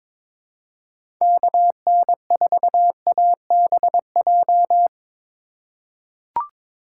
Morse code, heard as a keyed tone, about 23 wpm, 700 Hz. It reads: KN4ABJ
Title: USA Call Signs - 6 Characters - 2x Character Spacing - 22wpm